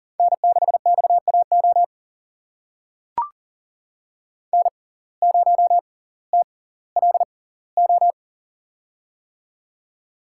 Morse code, heard as a keyed tone, about 37 wpm, spaced as for 20 wpm, 700 Hz. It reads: N6XAO N0TLO
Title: USA Call Signs - 5 Characters - 6x Character Spacing - 40wpm